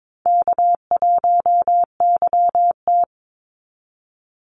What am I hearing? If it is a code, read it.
Morse code, heard as a keyed tone, about 22 wpm, 690 Hz. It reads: K1YT
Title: Call Signs Repeated 3x - 22wpm